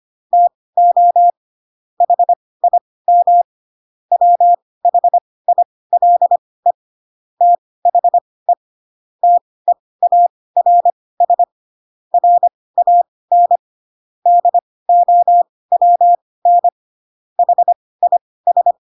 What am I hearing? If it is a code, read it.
Morse code, heard as a keyed tone, about 25 wpm, spaced as for 18 wpm, 700 Hz. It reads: TO HIM WHILE THE TEARS RAN DOWN HIS
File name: Pinocchio / Pinocchio ch7 12 25 ik1hge